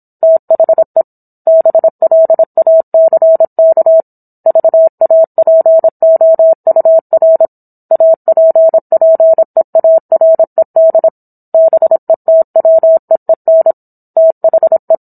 Morse code, heard as a keyed tone, about 27 wpm, 650 Hz. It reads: THE BLACK VAPOUR APPEARED BETWEEN THE